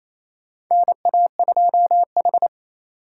Morse code, heard as a keyed tone, about 27 wpm, 700 Hz. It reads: NA2H
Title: USA Call Signs - 3x Character Spacing - 28wpm